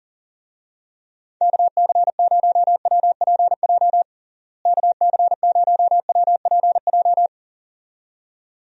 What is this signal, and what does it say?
Morse code, heard as a keyed tone, about 40 wpm, 700 Hz. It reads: KC0WPJ KC0WPJ
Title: USA Call Signs - 6 Characters - Repeated 2x - 40wpm